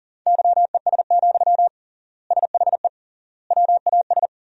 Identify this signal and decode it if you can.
Morse code, heard as a keyed tone, about 40 wpm, 700 Hz. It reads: YES, SHE WAS
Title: 3x Word-Spacing - Sentences from Top 500 Words - 40wpm